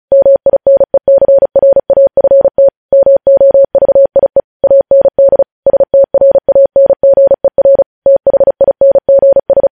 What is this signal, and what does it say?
Morse code, heard as a keyed tone, about 36 wpm, 550 Hz. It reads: MINECRAFT MOVIE AND STRANGER THINGS
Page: News headlines in Morse code at 35 wpm